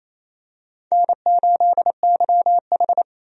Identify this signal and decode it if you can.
Morse code, heard as a keyed tone, about 28 wpm, 700 Hz. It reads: N8YH
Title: USA Call Signs - 4 Characters - Repeated 2x - 28wpm